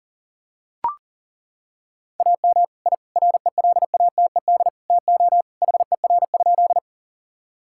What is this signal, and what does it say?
Morse code, heard as a keyed tone, about 40 wpm, 700 Hz. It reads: AM I RELATED TO HER?
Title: Sentences from Top 1000 Words - Review Part 5 - 40wpm